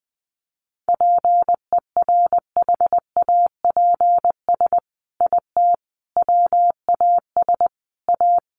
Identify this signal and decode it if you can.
Morse code, heard as a keyed tone, about 20 wpm, 700 Hz. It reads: PERHAPS IT WAS A
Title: Sentences from Top 1400 Words 20wpm